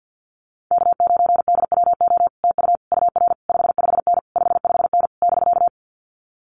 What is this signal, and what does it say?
Morse code, heard as a keyed tone, about 50 wpm, 700 Hz. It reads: K9DWO TU UR 55N 55N <BK>
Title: Common QSO Phrases Encore 50wpm